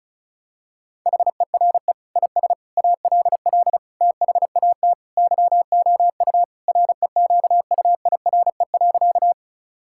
Morse code, heard as a keyed tone, about 36 wpm, 700 Hz. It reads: HERE IS ALL THAT YOU REQUIRE.
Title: Sentences from Top 1000 Words - Review Part 6 - 35wpm